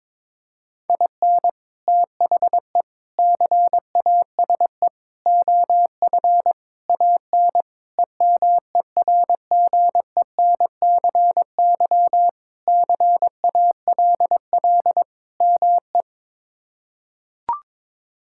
Morse code, heard as a keyed tone, about 22 wpm, 700 Hz. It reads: IN THE CASE OF AN EMERGENCY CALL ME
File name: practice / 22wpm / Sentences with 3-Word Phrases - 22wpm